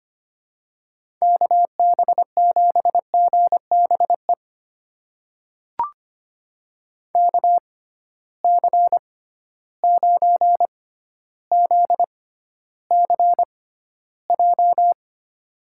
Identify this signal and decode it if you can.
Morse code, heard as a keyed tone, about 25 wpm, spaced as for 13 wpm, 700 Hz. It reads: KB7GBE KC9ZCJ
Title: USA Call Signs - 6 Characters - 6x Character Spacing - 25wpm